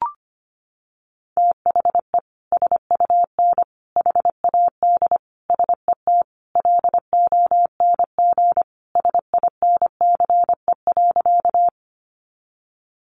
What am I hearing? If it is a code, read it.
Morse code, heard as a keyed tone, about 25 wpm, 700 Hz. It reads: THE SUN HAD SET LONG SINCE.
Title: Sentences from Top 300 Words 25wpm